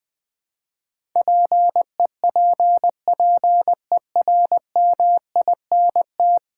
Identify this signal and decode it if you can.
Morse code, heard as a keyed tone, about 20 wpm, 700 Hz. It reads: PEPPERMINT
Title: Difficult Words with Hints - 20wpm